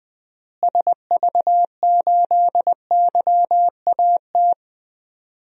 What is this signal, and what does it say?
Morse code, heard as a keyed tone, about 21 wpm, 700 Hz. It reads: SV8YAT
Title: DXCC-100 Call Signs - 6 Characters - Repeated 2x - 20wpm